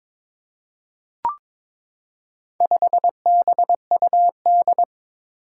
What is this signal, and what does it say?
Morse code, heard as a keyed tone, about 22 wpm, 700 Hz. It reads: 5BUD
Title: Four Challenging Letter-Number Combinations 22wpm